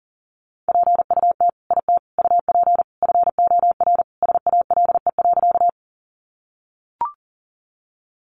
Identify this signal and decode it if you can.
Morse code, heard as a keyed tone, about 37 wpm, 700 Hz. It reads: PUT IT UP FOR SALE.
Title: Sentences from Top 1400 Words 40wpm